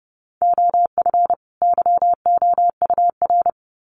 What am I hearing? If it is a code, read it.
Morse code, heard as a keyed tone, about 30 wpm, 700 Hz. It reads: OF YOUR
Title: Sets of 2 Words 30wpm